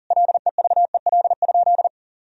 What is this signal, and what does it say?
Morse code, heard as a keyed tone, about 40 wpm, 700 Hz. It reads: LEVEL?
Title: Sentences from Top 1000 Words - Review Part 4 - 40wpm